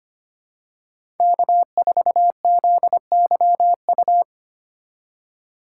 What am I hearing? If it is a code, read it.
Morse code, heard as a keyed tone, about 25 wpm, 700 Hz. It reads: K4ZYU
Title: USA Call Signs - 5 Characters - 3x Character Spacing - 25wpm